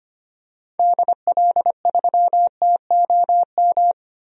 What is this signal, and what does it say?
Morse code, heard as a keyed tone, about 25 wpm, 700 Hz. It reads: DL3TOM
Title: DXCC-100 Call Signs - 6 Characters - 25wpm